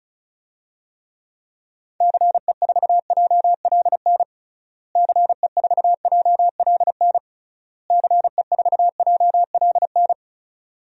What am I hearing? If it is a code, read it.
Morse code, heard as a keyed tone, about 36 wpm, 700 Hz. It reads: CE4JLN CE4JLN CE4JLN
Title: DXCC-100 Call Signs - 6 Characters - Repeated 3x - 35wpm